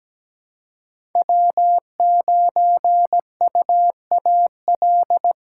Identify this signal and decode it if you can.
Morse code, heard as a keyed tone, about 17 wpm, 700 Hz. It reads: W9UAL
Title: USA Callsign DE Callsign - 5 Characters - 17wpm